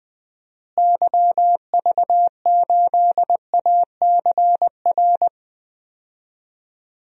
Morse code, heard as a keyed tone, about 20 wpm, 700 Hz. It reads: YV8ACR
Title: DXCC-100 Call Signs - 6 Characters - Repeated 3x - 20wpm